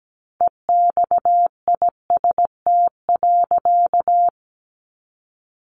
Morse code, heard as a keyed tone, about 17 wpm, 700 Hz. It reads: EXIST.
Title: A Sound Of Thunder 17wpm